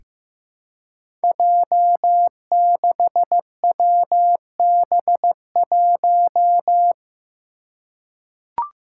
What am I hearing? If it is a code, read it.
Morse code, heard as a keyed tone, about 15 wpm, 700 Hz. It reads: J6WB1
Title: Five Challenging Letter-Number Combinations 15wpm